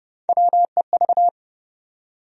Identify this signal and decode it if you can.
Morse code, heard as a keyed tone, about 29 wpm, 700 Hz. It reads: WEV
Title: USA Call Sign Postfixes - 3 Letters 30wpm